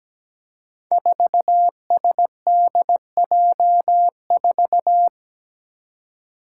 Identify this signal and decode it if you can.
Morse code, heard as a keyed tone, about 17 wpm, 700 Hz. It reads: 4SDJ4
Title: Five Challenging Letter-Number Combinations 17wpm